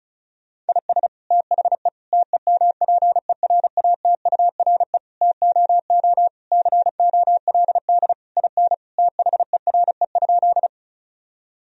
Morse code, heard as a keyed tone, about 34 wpm, 700 Hz. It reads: IS THE TEMPERATURE TOO COLD IN THERE?
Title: Sentences from Top 1000 Words - Review Part 4 - 35wpm